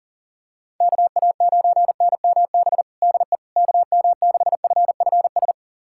Morse code, heard as a keyed tone, about 40 wpm, 700 Hz. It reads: KA9NMB DE KM6FFS
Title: USA Callsign DE Callsign 40wpm